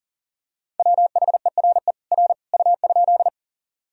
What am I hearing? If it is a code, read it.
Morse code, heard as a keyed tone, about 37 wpm, 700 Hz. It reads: WHERE R U?
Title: Sentences from Top 200 Words and Abbreviations 40wpm